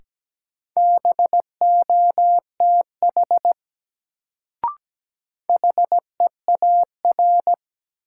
Morse code, heard as a keyed tone, about 17 wpm, 700 Hz. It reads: BOTH HEAR
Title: Sentences from Top 300 Words - New Word Drill - 17wpm